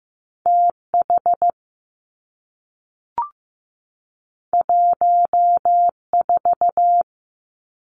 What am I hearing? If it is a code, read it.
Morse code, heard as a keyed tone, about 15 wpm, 700 Hz. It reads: T H 1 4
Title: Two Challenging Letter-Number Combinations 15wpm